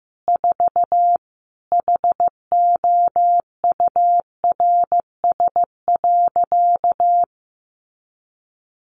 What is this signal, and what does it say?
Morse code, heard as a keyed tone, about 15 wpm, 700 Hz. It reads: 4 HOURS.
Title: Additional Easy Phrases 15wpm